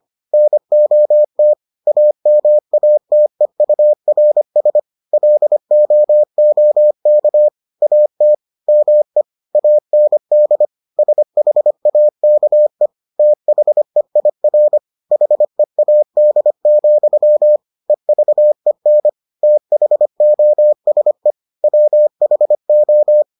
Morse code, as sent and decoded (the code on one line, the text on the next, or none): -. --- - / .- -- .- - . ..- .-. ... / .-.. --- --- -.- / .- - / -- . / .- -. -.. / ... .... .- -.- . / - .... .. .-. / .... .- -.. --..-- / . ...- . -. / - .... --- ... / .-- .... ---
NOT AMATEURS LOOK AT ME AND SHAKE THEIR HEAD, EVEN THOSE WHO